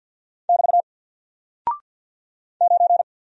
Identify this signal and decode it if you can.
Morse code, heard as a keyed tone, about 48 wpm, 700 Hz. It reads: <BT> 9
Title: Single Character (Letter:Number:Punctuation:Prosign) Alt 50wpm